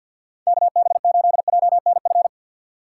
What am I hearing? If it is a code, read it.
Morse code, heard as a keyed tone, about 53 wpm, 700 Hz. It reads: KB8JNF
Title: USA Call Signs - 6 Characters - 6x Character Spacing - 50wpm